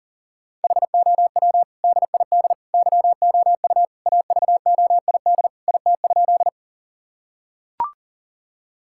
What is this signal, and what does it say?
Morse code, heard as a keyed tone, about 40 wpm, 700 Hz. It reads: HOW DID YOU AVOID IT?
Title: Sentences from Top 1000 Words - Review Part 6 - 40wpm